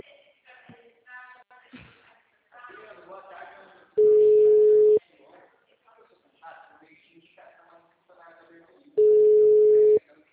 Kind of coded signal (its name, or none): telephone call progress tone